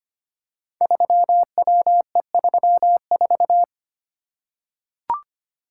Morse code, heard as a keyed tone, about 25 wpm, 700 Hz. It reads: 3WE34
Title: Code Groups - 5 Letter:Numbers - 25wpm